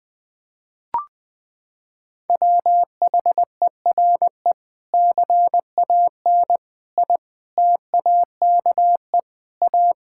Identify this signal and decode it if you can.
Morse code, heard as a keyed tone, about 20 wpm, 700 Hz. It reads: WHERE CAN I TAKE A
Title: Sentences from Top 1600 Words 20wpm